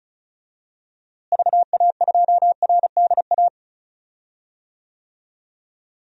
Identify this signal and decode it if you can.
Morse code, heard as a keyed tone, about 36 wpm, 700 Hz. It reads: VA2RDA
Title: DXCC-100 Call Signs - 6 Characters - Repeated 3x - 35wpm